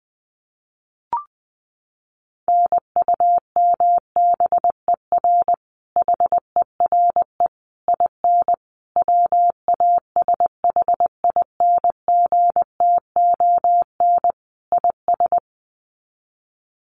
Morse code, heard as a keyed tone, about 20 wpm, 700 Hz. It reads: NUMBER HERE IN WASHINGTON IS
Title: Sets of 5 Words Encore2 20wpm